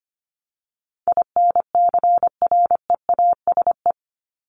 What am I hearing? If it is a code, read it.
Morse code, heard as a keyed tone, about 25 wpm, 700 Hz. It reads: INCREASE
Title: LY Suffix - Top 100 Words with Base Words 25wpm